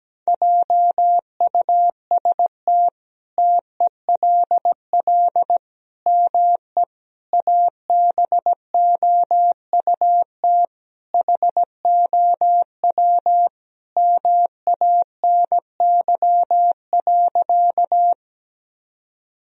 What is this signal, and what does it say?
Morse code, heard as a keyed tone, about 17 wpm, 700 Hz. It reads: JUST TELL ME ABOUT HOW MANY.
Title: Sentences from Top 1000 Words - Review Part 1 - 17wpm